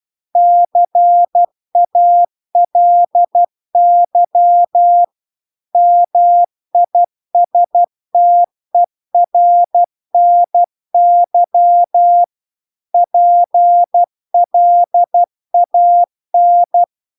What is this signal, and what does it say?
Morse code, heard as a keyed tone, about 12 wpm, 700 Hz. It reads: CALY MISTERNY PLAN